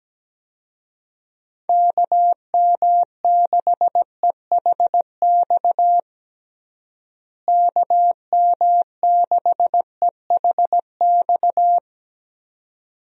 Morse code, heard as a keyed tone, about 17 wpm, 700 Hz. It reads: KM6EHX KM6EHX
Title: USA Call Signs - 6 Characters - Repeated 2x - 17wpm